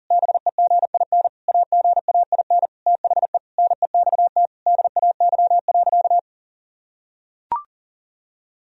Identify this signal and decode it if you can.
Morse code, heard as a keyed tone, about 40 wpm, 700 Hz. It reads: BEGIN AGAIN THE NEXT DAY.
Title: Sentences from Top 1000 Words - Review Part 1 - 40wpm